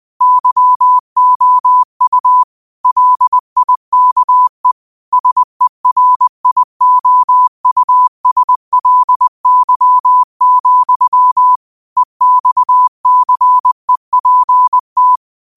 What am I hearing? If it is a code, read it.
Morse code, heard as a keyed tone, about 20 wpm, 1000 Hz. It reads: YOU LIKE SERIOUSLY, EXCEPT